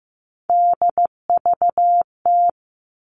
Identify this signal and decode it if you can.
Morse code, heard as a keyed tone, about 15 wpm, 700 Hz. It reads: DVT
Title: All Cut Numbers - 3-Digit 15wpm